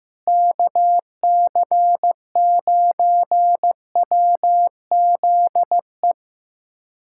Morse code, heard as a keyed tone, about 15 wpm, 700 Hz. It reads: KC9WZE
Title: USA Callsign DE Callsign - 6 Characters - 15wpm